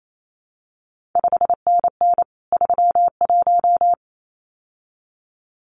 Morse code, heard as a keyed tone, about 28 wpm, 700 Hz. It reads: 5NN 31
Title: CQ WW DX Contest Exchange - 28wpm